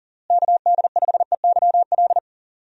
Morse code, heard as a keyed tone, about 40 wpm, 700 Hz. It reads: KD5EYL
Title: USA Call Signs - 6 Characters - Repeated 3x - 40wpm